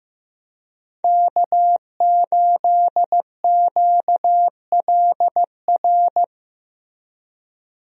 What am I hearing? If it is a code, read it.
Morse code, heard as a keyed tone, about 15 wpm, 700 Hz. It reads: K8QLR
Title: USA Call Signs - 5 Characters - Repeated 3x - 15wpm